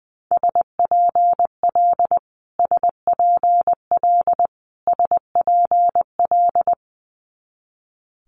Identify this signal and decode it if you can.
Morse code, heard as a keyed tone, about 20 wpm, 700 Hz. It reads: SPL SPL SPL
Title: Multi-letter Phonemes Repeated 3x 20wpm